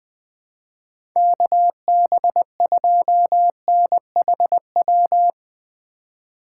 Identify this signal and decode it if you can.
Morse code, heard as a keyed tone, about 20 wpm, 700 Hz. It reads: KB2NHW
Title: USA Call Signs - 6 Characters - 4x Character Spacing - 20wpm